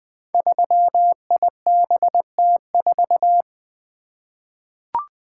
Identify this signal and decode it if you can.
Morse code, heard as a keyed tone, about 20 wpm, 700 Hz. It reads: 3IBT4
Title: Code Groups - 5 Letter:Numbers - 20wpm